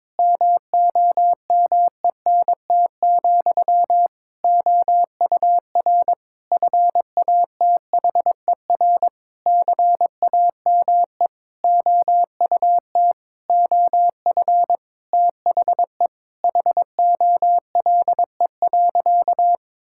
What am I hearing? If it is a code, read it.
Morse code, heard as a keyed tone, about 22 wpm, 700 Hz. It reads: MOMENT, OUR FATHER CAME OUT OF THE HOLE.
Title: Sentences from Top 1600 Words 22wpm